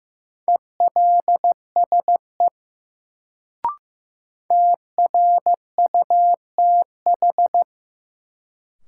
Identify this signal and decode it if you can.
Morse code, heard as a keyed tone, about 15 wpm, 700 Hz. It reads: ELSE TRUTH
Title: Sentences from Top 800 Words - New Word Drill - 15wpm